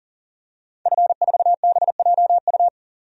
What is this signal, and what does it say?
Morse code, heard as a keyed tone, about 38 wpm, 700 Hz. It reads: F4BJU
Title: DXCC-100 Call Signs - 5 Characters - 40wpm